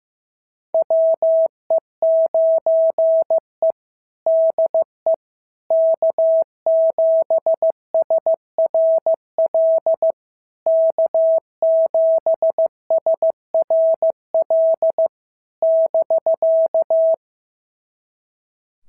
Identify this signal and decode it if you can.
Morse code, heard as a keyed tone, about 15 wpm, 650 Hz. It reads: WE9E DE K7SRL K7SRL <BK>
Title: SOTA QSO Practice 15wpm